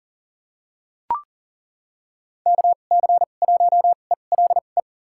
Morse code, heard as a keyed tone, about 40 wpm, spaced as for 29 wpm, 700 Hz. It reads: KC1ELE
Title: USA Call Signs - 6 Characters - 2x Character Spacing - 40wpm